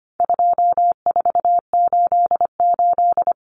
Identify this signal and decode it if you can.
Morse code, heard as a keyed tone, about 25 wpm, 700 Hz. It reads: 2488
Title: Numbers - 4-Digit 25wpm